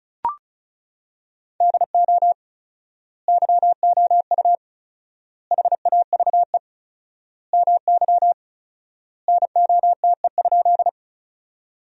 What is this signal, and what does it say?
Morse code, heard as a keyed tone, about 34 wpm, 700 Hz. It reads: DO YOU HAVE MY NOTE?
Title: 4x Word-Spacing - Sentences from Top 500 Words - 35wpm